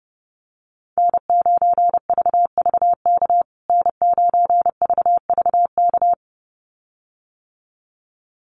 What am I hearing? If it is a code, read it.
Morse code, heard as a keyed tone, about 30 wpm, 700 Hz. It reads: N9VVK N9VVK
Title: USA Call Signs Repeated 2x 30wpm